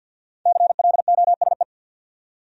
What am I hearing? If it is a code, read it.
Morse code, heard as a keyed tone, about 61 wpm, 700 Hz. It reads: CLOSE
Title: Speed-Racing Top 500 Words 50wpm